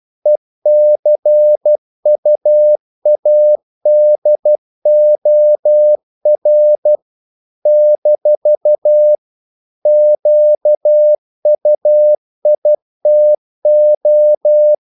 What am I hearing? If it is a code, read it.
Morse code, heard as a keyed tone, about 12 wpm, 600 Hz. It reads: ECUADOR - QUITO
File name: CW / Practice / NationalCapitals12